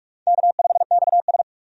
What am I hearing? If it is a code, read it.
Morse code, heard as a keyed tone, about 46 wpm, 700 Hz. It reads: K5XS
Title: USA Call Signs - 4 Characters - 45wpm